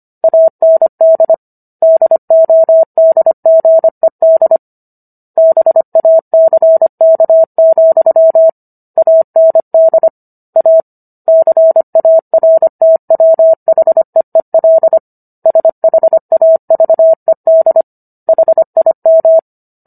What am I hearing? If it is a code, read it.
Morse code, heard as a keyed tone, about 25 wpm, 650 Hz. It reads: AND DODGED BACK, AND A CARTWHEEL SHAVED HIM